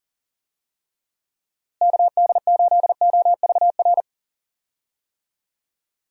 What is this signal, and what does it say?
Morse code, heard as a keyed tone, about 40 wpm, 700 Hz. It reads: KD8OVR
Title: USA Call Signs - 6 Characters - Repeated 2x - 40wpm